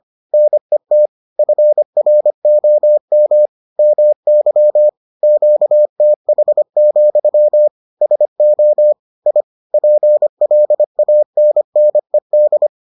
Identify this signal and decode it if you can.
Morse code, heard as a keyed tone, about 25 wpm, 600 Hz. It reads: NET FROM MY QTH, SO I PLANNED